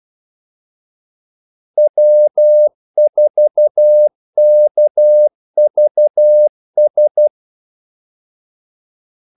Morse code, heard as a keyed tone, about 12 wpm, 600 Hz. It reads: W4KVS
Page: 2 ekstra ordmellemrum mellem kaldesignalerne